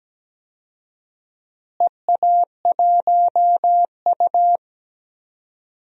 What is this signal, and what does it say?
Morse code, heard as a keyed tone, about 17 wpm, 700 Hz. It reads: EA1U
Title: DXCC-100 Call Signs - 4 Characters - Repeated 2x - 17wpm